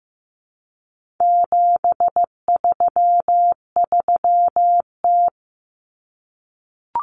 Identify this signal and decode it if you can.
Morse code, heard as a keyed tone, about 15 wpm, 700 Hz. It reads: 733T
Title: Common Cut Numbers - 4-Digit 15wpm